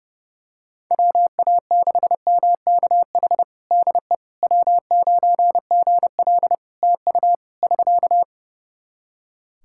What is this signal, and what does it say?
Morse code, heard as a keyed tone, about 30 wpm, 700 Hz. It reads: WA6MKH DE W9GL TU <SK>
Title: Practice QSO - Exchange 4 - Ending - 30wpm